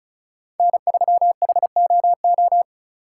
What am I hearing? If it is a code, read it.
Morse code, heard as a keyed tone, about 34 wpm, 700 Hz. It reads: N3HOO
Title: USA Call Signs - 5 Characters - 4x Character Spacing - 35wpm